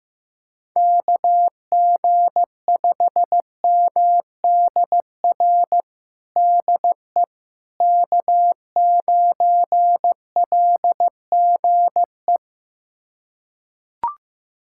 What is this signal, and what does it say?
Morse code, heard as a keyed tone, about 16 wpm, 700 Hz. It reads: KG5MDR DE K9LGE
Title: USA Callsign DE Callsign 15wpm